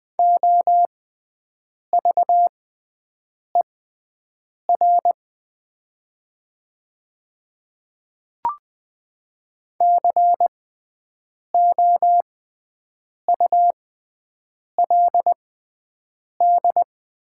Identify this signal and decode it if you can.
Morse code, heard as a keyed tone, about 20 wpm, spaced as for 7 wpm, 700 Hz. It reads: OVER COULD
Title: 6x Character Spacing - Top 100 Words - 20wpm